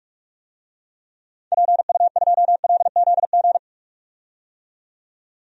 Morse code, heard as a keyed tone, about 46 wpm, 700 Hz. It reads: PU2LZG
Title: DXCC-100 Call Signs - 6 Characters - 45wpm